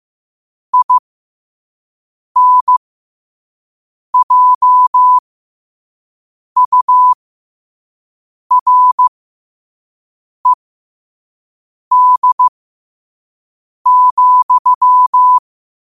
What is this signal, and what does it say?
Morse code, heard as a keyed tone, about 15 wpm, spaced as for 5 wpm, 1000 Hz. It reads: INJURED,
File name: qotdfiles / 5wpm